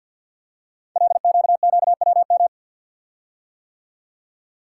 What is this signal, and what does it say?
Morse code, heard as a keyed tone, about 48 wpm, 700 Hz. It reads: LQQWM
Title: Code Groups - 5 Letters - 50wpm